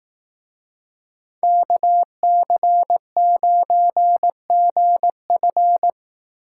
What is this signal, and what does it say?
Morse code, heard as a keyed tone, about 18 wpm, 700 Hz. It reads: KC9GF
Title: Speed Racing - USA 2x2 Call Signs 15wpm